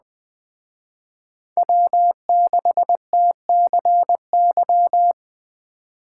Morse code, heard as a keyed tone, about 20 wpm, 700 Hz. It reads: W6TCY
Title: Speed Racing - USA 1x3 Call Signs 17wpm